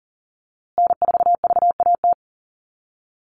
Morse code, heard as a keyed tone, about 40 wpm, 700 Hz. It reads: N4VAT
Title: USA Call Signs Repeated 3x 40wpm